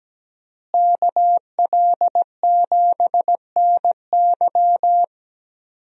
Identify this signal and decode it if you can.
Morse code, heard as a keyed tone, about 17 wpm, 700 Hz. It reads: KL7NY
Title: USA Call Signs - 5 Characters - Repeated 3x - 17wpm